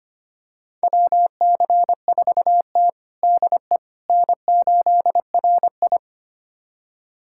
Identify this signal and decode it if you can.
Morse code, heard as a keyed tone, about 25 wpm, 700 Hz. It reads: WC4T DE N8RI
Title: USA Callsign DE Callsign - 4 Characters - 25wpm